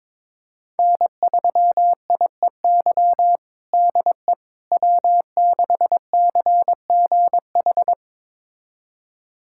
Morse code, heard as a keyed tone, about 22 wpm, 700 Hz. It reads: N3IEY DE W6CGH
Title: USA Callsign DE Callsign - 5 Characters - 22wpm